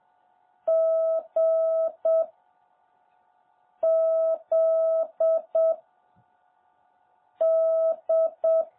Morse code, heard as a keyed tone, about 7 wpm, 650 Hz. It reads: G Z D
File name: jh8cbh / 2021cw